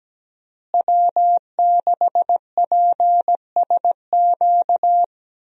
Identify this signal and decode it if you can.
Morse code, heard as a keyed tone, about 17 wpm, 700 Hz. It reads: W6PSQ
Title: USA Call Signs - 5 Characters - Repeated 3x - 17wpm